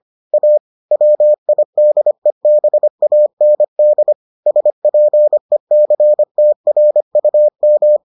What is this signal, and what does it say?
Morse code, heard as a keyed tone, about 25 wpm, 600 Hz. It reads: A WIDEBAND SPECTRUM